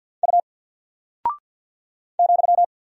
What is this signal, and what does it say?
Morse code, heard as a keyed tone, about 48 wpm, 700 Hz. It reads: U ,
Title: Single Character (Letter:Number:Punctuation:Prosign) Alt 50wpm